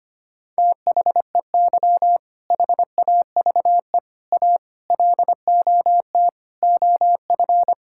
Morse code, heard as a keyed tone, about 25 wpm, 700 Hz. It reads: THEY HAVE A LOT OF
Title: Sentences from Top 1000 Words - Review Part 5 - 25wpm